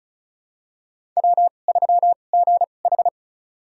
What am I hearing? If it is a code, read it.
Morse code, heard as a keyed tone, about 36 wpm, spaced as for 25 wpm, 690 Hz. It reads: W3GH
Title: USA Call Signs - 4 Characters - 2x Character Spacing - 35wpm